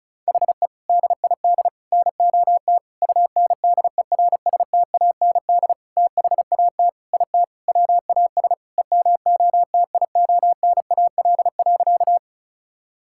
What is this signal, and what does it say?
Morse code, heard as a keyed tone, about 34 wpm, 700 Hz. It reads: HE DID NOT UNDERSTAND THAT IT WAS EMOTIONAL.
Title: Sentences from Top 1000 Words - Review Part 5 - 35wpm